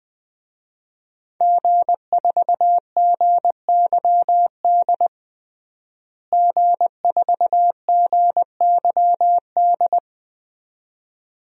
Morse code, heard as a keyed tone, about 21 wpm, 700 Hz. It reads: G4GYD G4GYD
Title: DXCC-100 Call Signs - 5 Characters - Repeated 2x - 20wpm